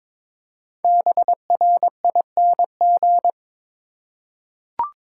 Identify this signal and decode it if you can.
Morse code, heard as a keyed tone, about 22 wpm, 690 Hz. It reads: BRING
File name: 5 Letter Words - First 3 Letters Given - 22wpm